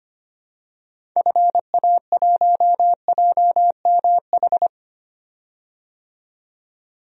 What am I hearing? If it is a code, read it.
Morse code, heard as a keyed tone, about 25 wpm, 700 Hz. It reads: FA1JMH
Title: DXCC-100 Call Signs - 6 Characters - 25wpm